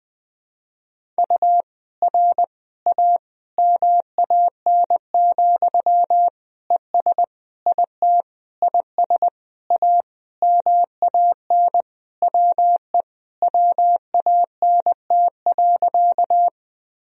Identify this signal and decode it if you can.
Morse code, heard as a keyed tone, about 20 wpm, 700 Hz. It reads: U R A MAN, ES IT IS A MAN WE WANT.
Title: Sentences from Top 200 Words and Abbreviations 20wpm